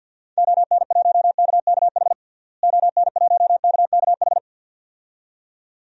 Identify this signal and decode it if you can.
Morse code, heard as a keyed tone, about 48 wpm, 700 Hz. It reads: ON1KKH ON1KKH
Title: DXCC-100 Call Signs - 6 Characters - Repeated 2x - 50wpm